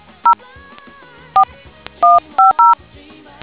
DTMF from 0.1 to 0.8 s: 0